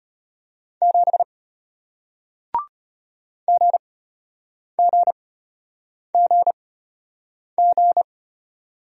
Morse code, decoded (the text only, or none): 7 GGGG